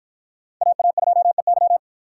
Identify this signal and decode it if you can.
Morse code, heard as a keyed tone, about 53 wpm, 700 Hz. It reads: AA2EY